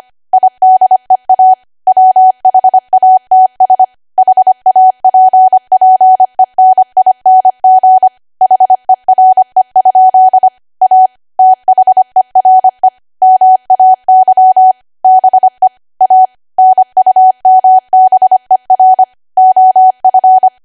Morse code, heard as a keyed tone, about 25 wpm, 750 Hz. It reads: IDEA WHATS HAPPENING HERE? A THERE MAY BE A NUMBER OF
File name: Archive / 25 WPM / 25WPM